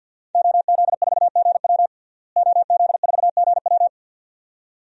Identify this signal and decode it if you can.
Morse code, heard as a keyed tone, about 51 wpm, 700 Hz. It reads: OZ4GW OZ4GW
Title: DXCC-100 Call Signs - 5 Characters - Repeated 2x - 50wpm